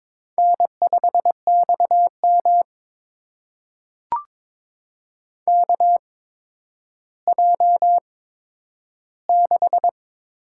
Morse code, heard as a keyed tone, about 22 wpm, spaced as for 11 wpm, 700 Hz. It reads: N5XM KJ6
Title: USA Call Signs - 8x Character Spacing - 22wpm